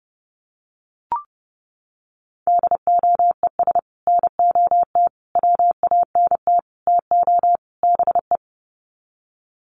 Morse code, heard as a keyed tone, about 30 wpm, 700 Hz. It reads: DOES NOT WANT TO BE
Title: Sets of 5 Words Encore 30wpm